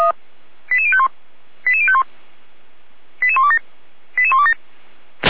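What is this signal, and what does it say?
Five-tone selective call (ZVEI), heard as quick runs of five tones, tones of about 70 ms. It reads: 88051 88051 88127 88127
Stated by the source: Five-tone ZVEI selective call received on SDR, from audio.f85360.feuerwehr-orlamuende.org (Feuerwehr Orlamünde, fire brigade).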